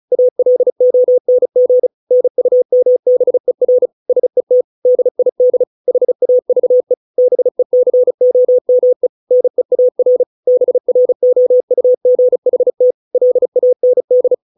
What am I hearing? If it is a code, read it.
Morse code, heard as a keyed tone, about 36 wpm, 500 Hz. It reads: ALONG NUMBER SET DID HAVE BECOME NEAR BROUGHT LAND